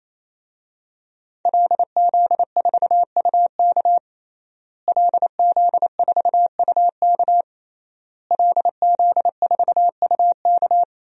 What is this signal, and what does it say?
Morse code, heard as a keyed tone, about 28 wpm, 700 Hz. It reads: LZ4UK LZ4UK LZ4UK
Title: DXCC-100 Call Signs - 5 Characters - Repeated 3x - 28wpm